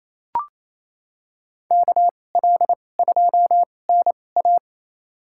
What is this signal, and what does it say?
Morse code, heard as a keyed tone, about 28 wpm, spaced as for 20 wpm, 700 Hz. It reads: KL2NA
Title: USA Call Signs - 5 Characters - 2x Character Spacing - 28wpm